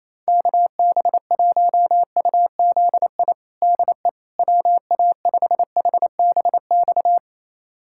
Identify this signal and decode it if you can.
Morse code, heard as a keyed tone, about 28 wpm, 700 Hz. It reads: KB1UZI DE WA5HBX
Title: USA Callsign DE Callsign 28wpm